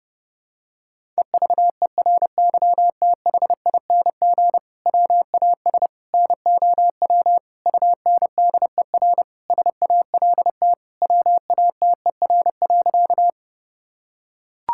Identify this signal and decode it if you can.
Morse code, heard as a keyed tone, about 29 wpm, 700 Hz. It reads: EVERYTHING WAS NOW UNDER SALT WATER.
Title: Sentences from Top 1000 Words - Review Part 5 - 30wpm